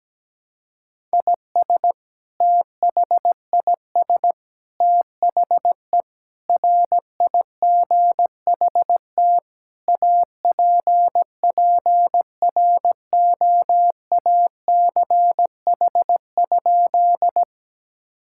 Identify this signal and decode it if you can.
Morse code, heard as a keyed tone, about 17 wpm, 700 Hz. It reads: IS THIS THE RIGHT APPROACH?
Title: Sentences from Top 1000 Words - Review Part 6 - 17wpm